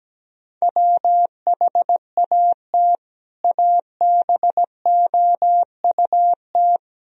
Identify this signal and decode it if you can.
Morse code, heard as a keyed tone, about 17 wpm, 700 Hz. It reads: WHAT ABOUT
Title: Sentences from Top 1000 Words - Review Part 1 - 17wpm